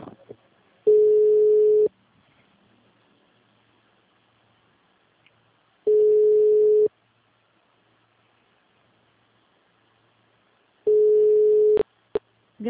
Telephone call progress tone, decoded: ringback tone

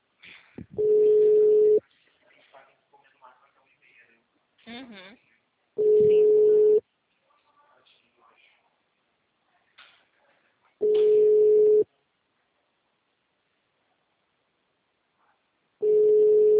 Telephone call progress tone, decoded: ringback tone

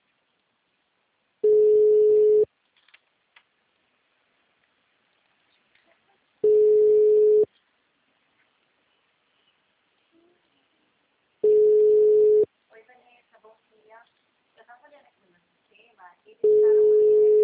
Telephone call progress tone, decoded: ringback tone

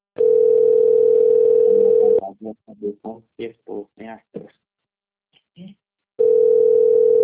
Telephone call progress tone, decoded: ringback tone

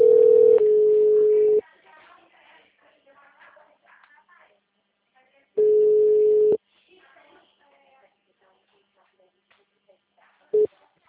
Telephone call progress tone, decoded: ringback tone

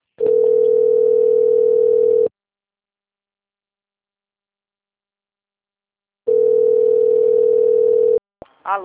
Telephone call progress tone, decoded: ringback tone

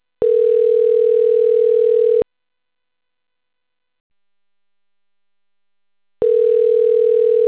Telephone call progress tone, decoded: ringback tone

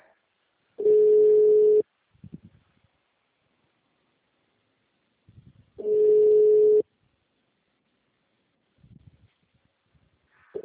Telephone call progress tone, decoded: ringback tone

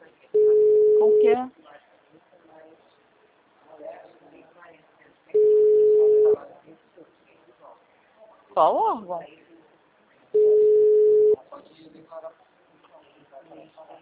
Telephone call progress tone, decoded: ringback tone

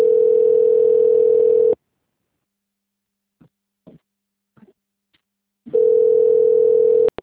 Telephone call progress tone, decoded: ringback tone

